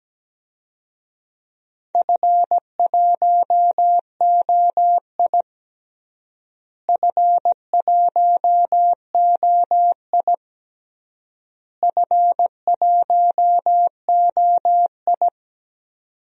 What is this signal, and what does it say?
Morse code, heard as a keyed tone, about 17 wpm, 700 Hz. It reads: F1OI F1OI F1OI
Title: DXCC-100 Call Signs - 4 Characters - Repeated 3x - 17wpm